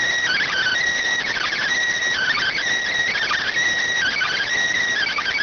Emergency Alert System (SAME) tones (header bursts), heard as harsh buzzing bursts. It is heard from 0.2 to 4.5 s.